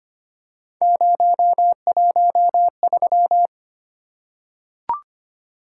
Morse code, heard as a keyed tone, about 25 wpm, 700 Hz. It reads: 013
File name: Three Letter-Number Combinations - 25wpm